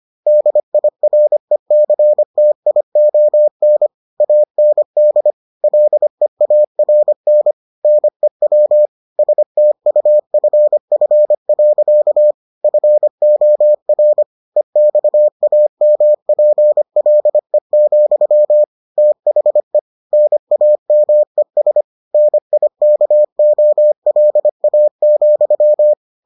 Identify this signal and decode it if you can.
Morse code, heard as a keyed tone, about 25 wpm, 600 Hz. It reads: DIRECTION AND LEARN NEW STUFF. FOR EXAMPLE, THE NAMES NIKOLA,